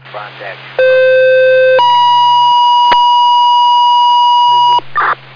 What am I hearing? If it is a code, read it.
Two-tone page, heard as a one-second tone, then a three-second tone, about 520 then 970 Hz, with speech. It is heard from 0.8 to 4.8 s.